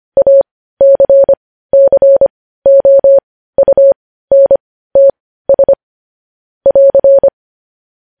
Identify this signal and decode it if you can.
Morse code, heard as a keyed tone, about 25 wpm, spaced as for 15 wpm, 550 Hz. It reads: ACCOUNTS <AR>
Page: news-headlines-in-morse-code-at-15-wpm